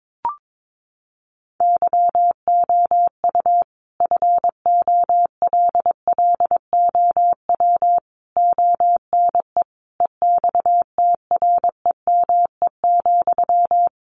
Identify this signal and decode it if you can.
Morse code, heard as a keyed tone, about 22 wpm, 700 Hz. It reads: YOU FOLLOW ONE EXTREME,
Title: Sentences from Top 1300 Words 22wpm